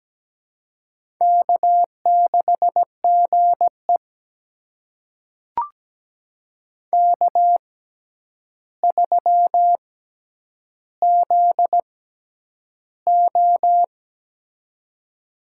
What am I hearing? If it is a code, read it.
Morse code, heard as a keyed tone, about 17 wpm, spaced as for 9 wpm, 700 Hz. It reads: K6GE K3ZO
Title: USA Call Signs - 4 Characters - 6x Character Spacing - 17wpm